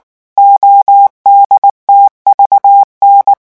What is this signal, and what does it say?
Morse code, heard as a keyed tone, about 19 wpm, 800 Hz. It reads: ODTVN
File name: morse mp3 / revision10-vitesse 18 mots